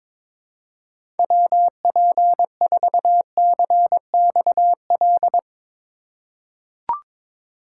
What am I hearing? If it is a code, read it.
Morse code, heard as a keyed tone, about 22 wpm, 700 Hz. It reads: WP4CXL
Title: USA Call Signs - 6 Characters - 3x Character Spacing - 22wpm